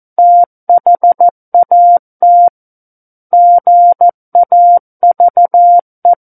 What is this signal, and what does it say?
Morse code, heard as a keyed tone, about 14 wpm, 700 Hz. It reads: THAT GAVE
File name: Book One / 05-Chapter 5 - 14 WPM